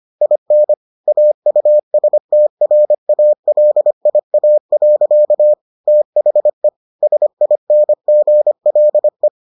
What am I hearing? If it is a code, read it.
Morse code, heard as a keyed tone, about 25 wpm, 600 Hz. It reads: IN AUSTRALIA. THE SINGLE